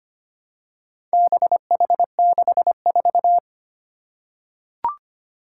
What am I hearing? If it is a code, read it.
Morse code, heard as a keyed tone, about 25 wpm, 700 Hz. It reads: BH64
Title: Four Challenging Letter-Number Combinations 25wpm